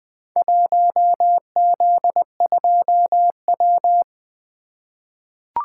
Morse code, heard as a keyed tone, about 20 wpm, 700 Hz. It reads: 1Z2W
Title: Four Letter-Number Combinations - 20wpm